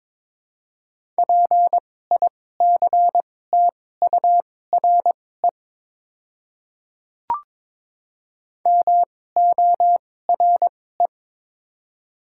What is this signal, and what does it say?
Morse code, heard as a keyed tone, about 22 wpm, spaced as for 16 wpm, 700 Hz. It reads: PICTURE MORE
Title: 2x Character Spacing - Top 200 Words - 22wpm